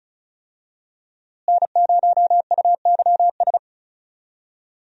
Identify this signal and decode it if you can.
Morse code, heard as a keyed tone, about 36 wpm, 700 Hz. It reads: N0UYS
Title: USA Call Signs - 5 Characters - Repeated 3x - 35wpm